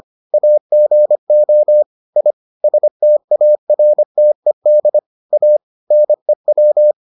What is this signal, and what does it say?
Morse code, heard as a keyed tone, about 25 wpm, 600 Hz. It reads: AGO I STARTED A NEW